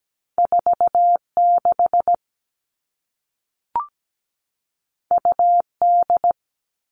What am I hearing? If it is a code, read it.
Morse code, heard as a keyed tone, about 17 wpm, 700 Hz. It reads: 46 UD
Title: All Cut Numbers - 2-Digit 17wpm